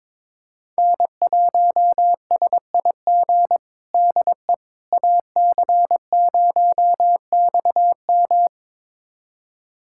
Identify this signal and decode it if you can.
Morse code, heard as a keyed tone, about 22 wpm, 700 Hz. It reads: N1SIG DE AC0XM
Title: USA Callsign DE Callsign - 5 Characters - 22wpm